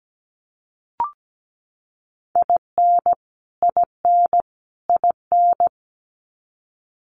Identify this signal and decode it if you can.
Morse code, heard as a keyed tone, about 17 wpm, 700 Hz. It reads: IN IN IN
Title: Top 300 Words Repeated 17wpm